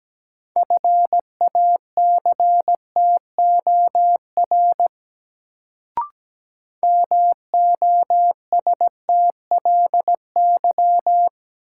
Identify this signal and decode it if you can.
Morse code, heard as a keyed tone, about 17 wpm, 700 Hz. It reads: FACTOR MOSTLY
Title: Sentences from Top 1300 Words - New Word Drill Alt - 17wpm